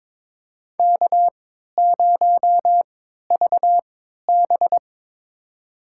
Morse code, heard as a keyed tone, about 23 wpm, spaced as for 13 wpm, 700 Hz. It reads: K0VB
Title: USA Call Signs - 4 Characters - 3x Character Spacing - 22wpm